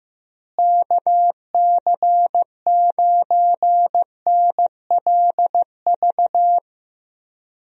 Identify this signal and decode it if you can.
Morse code, heard as a keyed tone, about 15 wpm, 700 Hz. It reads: KC9NLV
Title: USA Call Signs - 6 Characters - Repeated 3x - 15wpm